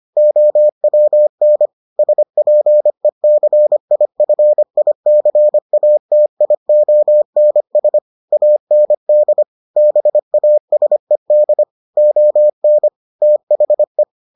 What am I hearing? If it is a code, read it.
Morse code, heard as a keyed tone, about 25 wpm, 600 Hz. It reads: OWN SPECIFICATIONS AND BASED ON THE